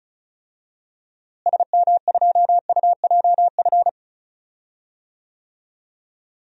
Morse code, heard as a keyed tone, about 36 wpm, 700 Hz. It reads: SM2UJF
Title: DXCC-100 Call Signs - 6 Characters - Repeated 2x - 35wpm